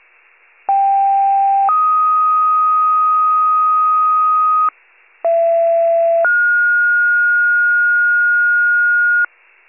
Two-tone page, heard as a one-second tone, then a three-second tone, about 790 then 1250 Hz, about 670 then 1430 Hz, with noise. It is heard from 0.7 to 9.3 s.
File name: pager / 39 FIRE / wcalert